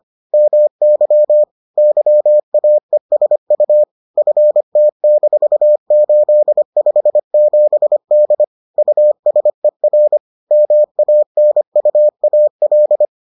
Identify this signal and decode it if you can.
Morse code, heard as a keyed tone, about 25 wpm, 600 Hz. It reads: MY YAESU FT-857D USER MANUAL